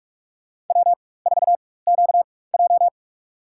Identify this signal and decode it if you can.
Morse code, heard as a keyed tone, about 43 wpm, spaced as for 21 wpm, 700 Hz. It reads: W4QJ
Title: USA Call Signs - 4x Character Spacing - 45wpm